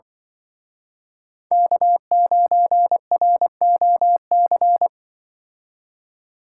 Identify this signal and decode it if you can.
Morse code, heard as a keyed tone, about 24 wpm, 700 Hz. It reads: K9ROC
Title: Speed Racing - USA 1x3 Call Signs 20wpm